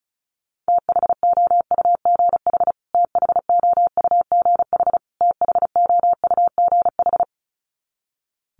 Morse code, heard as a keyed tone, about 34 wpm, 700 Hz. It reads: THOUGH THOUGH THOUGH
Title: Top 500 Words Repeated 35wpm